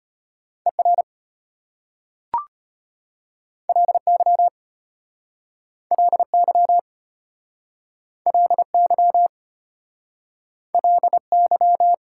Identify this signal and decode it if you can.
Morse code, heard as a keyed tone, about 34 wpm, 700 Hz. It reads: ER LY LY LY LY